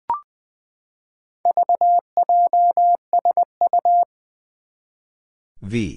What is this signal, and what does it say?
Morse code, heard as a keyed tone, about 20 wpm, 700 Hz. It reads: VJSU
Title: Four Challenging Letter-Number Combinations 20wpm